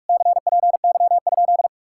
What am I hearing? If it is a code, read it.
Morse code, heard as a keyed tone, about 52 wpm, 700 Hz. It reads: CPY?
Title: Speed-Racing Common QSO Phrases - 45wpm